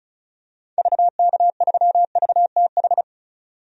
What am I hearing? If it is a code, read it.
Morse code, heard as a keyed tone, about 34 wpm, 700 Hz. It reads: VK3VTH
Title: DXCC-100 Call Signs - 6 Characters - 35wpm